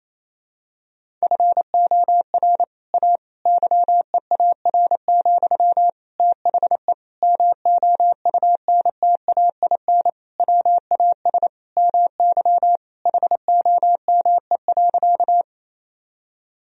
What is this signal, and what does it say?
Morse code, heard as a keyed tone, about 28 wpm, 700 Hz. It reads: FOR A YEAR, THE MOUNTAIN WAS MY HOME.
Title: Sentences from Top 1000 Words - Review Part 1 - 28wpm